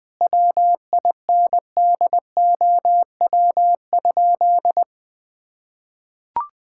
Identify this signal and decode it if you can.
Morse code, heard as a keyed tone, about 20 wpm, 700 Hz. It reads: WINDOW?
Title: Sentences from Top 1600 Words 20wpm